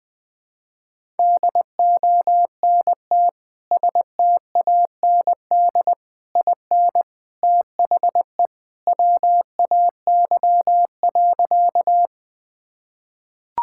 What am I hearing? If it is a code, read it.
Morse code, heard as a keyed tone, about 21 wpm, 700 Hz. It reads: DONT STAND IN THE WAY.
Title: Sentences with 3-Word Phrases - 20wpm